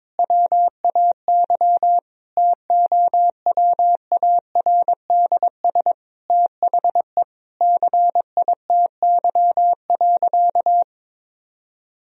Morse code, heard as a keyed tone, about 22 wpm, 700 Hz. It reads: WAY TOWARDS THE CITY.
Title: Sentences from Top 1000 Words - Review Part 2 - 22wpm